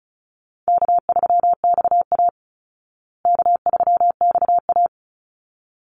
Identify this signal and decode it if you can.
Morse code, heard as a keyed tone, about 34 wpm, 700 Hz. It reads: K3XA K3XA
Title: Easy Call Signs Repeated 2x - 35wpm